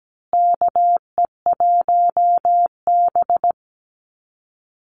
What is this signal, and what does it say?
Morse code, heard as a keyed tone, about 17 wpm, 700 Hz. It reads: KE1B
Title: Call Signs Repeated 2x - 17wpm